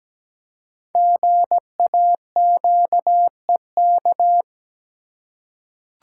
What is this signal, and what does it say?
Morse code, heard as a keyed tone, about 17 wpm, 700 Hz. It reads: GAQEK
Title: Code Groups - 5 Letters - 17wpm